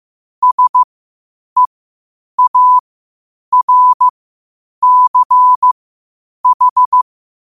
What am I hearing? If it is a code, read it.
Morse code, heard as a keyed tone, about 15 wpm, spaced as for 8 wpm, 1000 Hz. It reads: SEARCH
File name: qotdfiles / 8wpm